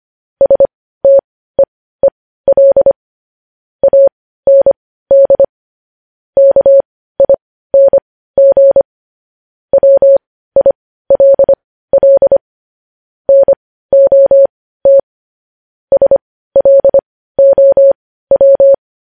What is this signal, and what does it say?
Morse code, heard as a keyed tone, about 25 wpm, spaced as for 15 wpm, 550 Hz. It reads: STEEL AND KING WILL NOT SLOW
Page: News Headlines in Morse Code at 15 WPM